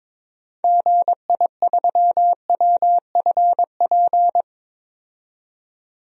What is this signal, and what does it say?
Morse code, heard as a keyed tone, about 22 wpm, 700 Hz. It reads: GI3WFP
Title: DXCC-100 Call Signs - 6 Characters - 22wpm